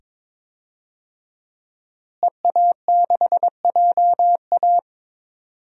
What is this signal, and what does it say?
Morse code, heard as a keyed tone, about 22 wpm, 700 Hz. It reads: EA6JA
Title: DXCC-100 Call Signs - 5 Characters - Repeated 2x - 22wpm